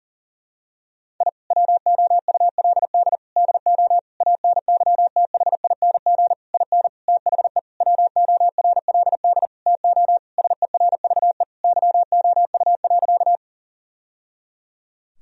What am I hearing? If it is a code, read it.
Morse code, heard as a keyed tone, about 37 wpm, 700 Hz. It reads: I WOULD DO ANYTHING IN THE WORLD TO SERVE YOU.
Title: Sentences from Top 1000 Words - Review Part 2 - 40wpm